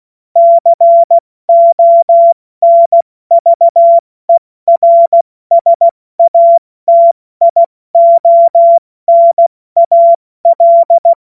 Morse code, heard as a keyed tone, about 16 wpm, 680 Hz. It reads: CONVERSATIONAL
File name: Morsekurs / Witze / Witz 23 Januar 2024 70 F80